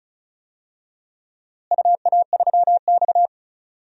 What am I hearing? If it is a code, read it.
Morse code, heard as a keyed tone, about 36 wpm, 700 Hz. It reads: UA3X
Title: DXCC-100 Call Signs - 4 Characters - Repeated 2x - 35wpm